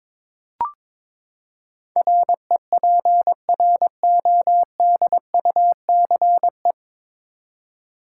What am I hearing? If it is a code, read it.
Morse code, heard as a keyed tone, about 22 wpm, 700 Hz. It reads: REPRODUCE
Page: Morse Code Ninja practice 22wpm